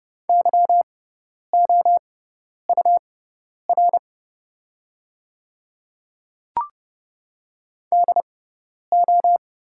Morse code, heard as a keyed tone, about 29 wpm, spaced as for 10 wpm, 700 Hz. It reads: YOUR DO
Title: 6x Character Spacing - Top 300 Words - 30wpm